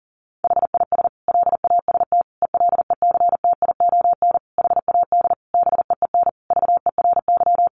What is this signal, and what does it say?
Morse code, heard as a keyed tone, about 37 wpm, 700 Hz. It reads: HIS LAST ELECTION HAD BEEN VERY
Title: Sentences from Top 1500 Words 40wpm